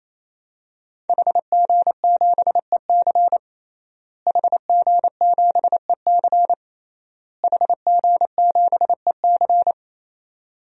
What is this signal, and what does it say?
Morse code, heard as a keyed tone, about 28 wpm, 700 Hz. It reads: HG7EC HG7EC HG7EC
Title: DXCC-100 Call Signs - 5 Characters - Repeated 3x - 28wpm